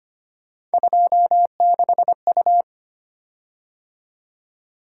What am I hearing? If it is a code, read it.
Morse code, heard as a keyed tone, about 25 wpm, 700 Hz. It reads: 26U
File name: Three Letter-Number Combinations - 25wpm